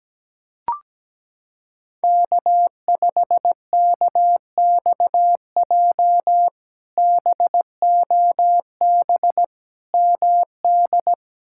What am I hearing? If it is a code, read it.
Morse code, heard as a keyed tone, about 17 wpm, 700 Hz. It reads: K5KXJ BOB MD
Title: K1USN SST QSO Practice - 17wpm